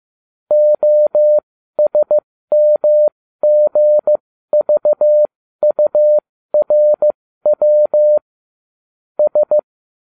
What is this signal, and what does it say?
Morse code, heard as a keyed tone, about 15 wpm, spaced as for 13 wpm, 600 Hz. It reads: OSMGVURW S